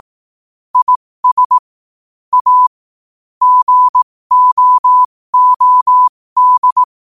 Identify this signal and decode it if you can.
Morse code, heard as a keyed tone, about 18 wpm, spaced as for 15 wpm, 1000 Hz. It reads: IS A GOOD
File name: 15wpm